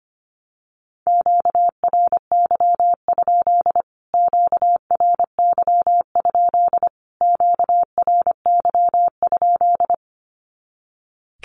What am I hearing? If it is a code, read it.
Morse code, heard as a keyed tone, about 25 wpm, 700 Hz. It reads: QRY? QRY? QRY?
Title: Q-Codes Repeated 3x 25wpm